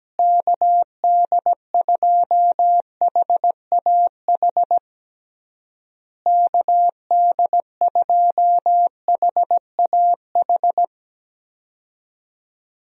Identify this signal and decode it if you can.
Morse code, heard as a keyed tone, about 17 wpm, 700 Hz. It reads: KD2HAH KD2HAH
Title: USA Call Signs - 6 Characters - Repeated 3x - 17wpm